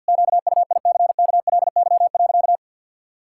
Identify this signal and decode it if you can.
Morse code, heard as a keyed tone, about 48 wpm, 700 Hz. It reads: QUICKLY.
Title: Pangram Challenge v2 - 50wpm